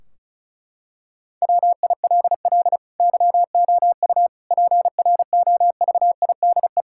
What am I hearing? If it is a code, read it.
Morse code, heard as a keyed tone, about 36 wpm, 700 Hz. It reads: WILL YOU PROVIDE
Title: Sentences from Top 1600 Words 35wpm